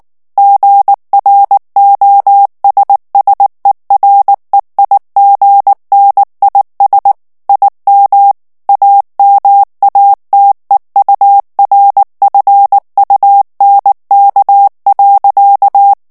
Morse code, heard as a keyed tone, about 19 wpm, 800 Hz. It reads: GROSSEREIGNIS IM AMATEURFUNK.
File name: morse mp3 / ALLEMAND texte clair 2 vitesse 18 mots minute